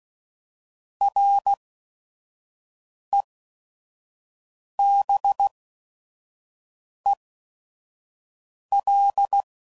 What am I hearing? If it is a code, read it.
Morse code, heard as a keyed tone, about 16 wpm, spaced as for 5 wpm, 800 Hz. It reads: REBEL
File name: Prefixes / By First Letter / Rs